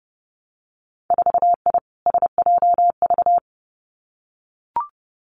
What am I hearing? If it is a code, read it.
Morse code, heard as a keyed tone, about 30 wpm, 700 Hz. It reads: 4I SJV
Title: Winter Field Day Exchanges Encore 30wpm